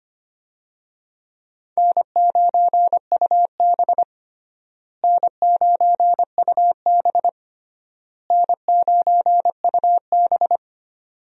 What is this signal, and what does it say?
Morse code, heard as a keyed tone, about 25 wpm, 700 Hz. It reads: N9UB N9UB N9UB
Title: USA Call Signs - 4 Characters - Repeated 3x - 25wpm